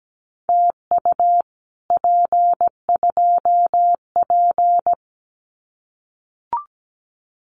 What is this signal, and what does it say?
Morse code, heard as a keyed tone, about 17 wpm, 700 Hz. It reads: TU P2P
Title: Common QSO Phrases Encore 17wpm